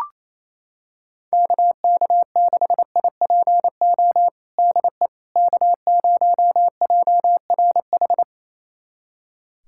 Morse code, heard as a keyed tone, about 28 wpm, 700 Hz. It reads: KK6IPO DE K0JRH
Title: USA Callsign DE Callsign 28wpm